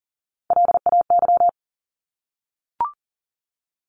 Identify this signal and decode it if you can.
Morse code, heard as a keyed tone, about 40 wpm, 700 Hz. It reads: LAY
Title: Top 500 Words 40wpm